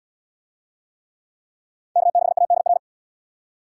Morse code, heard as a keyed tone, about 67 wpm, 700 Hz. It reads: K6ADR